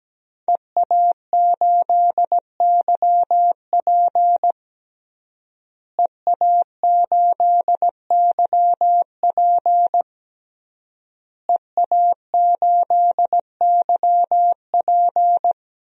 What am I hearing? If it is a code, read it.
Morse code, heard as a keyed tone, about 17 wpm, 700 Hz. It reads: EA8YP EA8YP EA8YP
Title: DXCC-100 Call Signs - 5 Characters - Repeated 3x - 17wpm